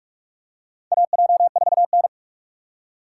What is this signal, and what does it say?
Morse code, heard as a keyed tone, about 44 wpm, 690 Hz. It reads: AJ4N
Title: USA Call Signs - 4 Characters - 2x Character Spacing - 45wpm